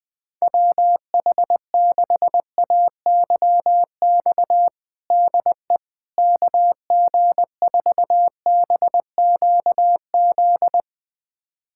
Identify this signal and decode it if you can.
Morse code, heard as a keyed tone, about 20 wpm, 700 Hz. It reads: WH6AYX DE KG4BQZ
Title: USA Callsign DE Callsign 20wpm